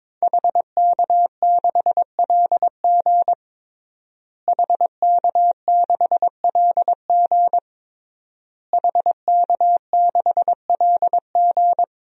Morse code, heard as a keyed tone, about 22 wpm, 700 Hz. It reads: HK6LG HK6LG HK6LG
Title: DXCC-100 Call Signs - 5 Characters - Repeated 3x - 22wpm